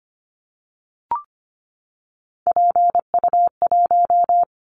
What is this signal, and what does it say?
Morse code, heard as a keyed tone, about 24 wpm, 700 Hz. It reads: PU1
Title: Callsign Prefixes for Top 100 DXCC - 25wpm